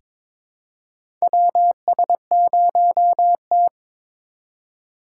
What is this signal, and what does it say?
Morse code, heard as a keyed tone, about 22 wpm, 700 Hz. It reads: WS0T
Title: USA Call Signs - 2x Character Spacing - 22wpm